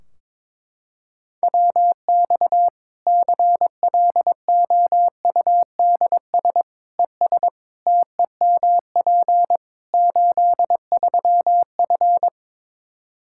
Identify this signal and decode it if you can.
Morse code, heard as a keyed tone, about 22 wpm, 700 Hz. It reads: WX CLOUDS ES TEMP 83F
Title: Practice QSO - Exchange 2 - Info - 22wpm